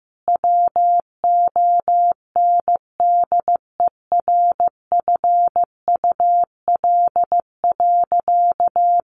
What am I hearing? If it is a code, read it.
Morse code, heard as a keyed tone, about 15 wpm, 700 Hz. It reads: WONDERFUL.
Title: Sentences from Top 1400 Words 15wpm